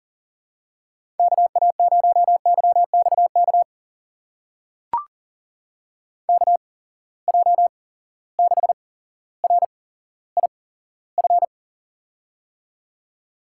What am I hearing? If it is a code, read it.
Morse code, heard as a keyed tone, about 39 wpm, spaced as for 17 wpm, 700 Hz. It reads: KA0YXK KJ6RIF
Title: USA Call Signs - 6 Characters - 8x Character Spacing - 40wpm